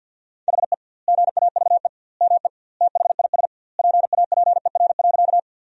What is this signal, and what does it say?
Morse code, heard as a keyed tone, about 50 wpm, 700 Hz. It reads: HE GAVE ME THIS PAPER.
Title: 2x Word-Spacing - Sentences from Top 500 Words - 50wpm